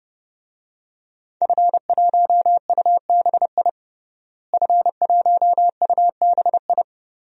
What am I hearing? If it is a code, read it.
Morse code, heard as a keyed tone, about 32 wpm, 700 Hz. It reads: F1UBI F1UBI
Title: DXCC-100 Call Signs - 5 Characters - Repeated 3x - 30wpm